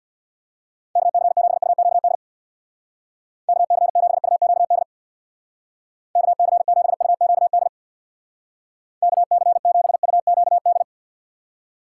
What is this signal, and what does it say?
Morse code, heard as a keyed tone, about 61 wpm, 690 Hz. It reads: KC7UQD KC7UQD KC7UQD KC7UQD
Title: Speed Racing - USA 2x3 Call Signs 50wpm